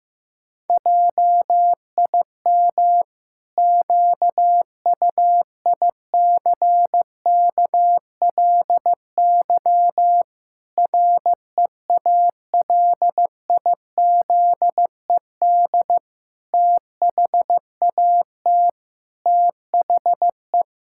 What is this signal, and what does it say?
Morse code, heard as a keyed tone, about 15 wpm, 700 Hz. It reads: JIM QUICKLY REALIZED THAT THE
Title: Pangram Challenge v2 - 15wpm